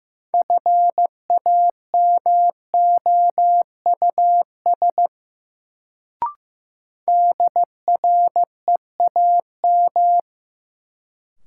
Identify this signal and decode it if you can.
Morse code, heard as a keyed tone, about 15 wpm, 700 Hz. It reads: FAMOUS DREAM
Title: Sentences from Top 1500 Words - New Word Drill - 15wpm